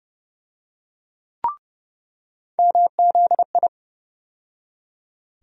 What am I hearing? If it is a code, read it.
Morse code, heard as a keyed tone, about 30 wpm, 700 Hz. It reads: MZI
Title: USA Call Sign Postfixes - 3 Letters 30wpm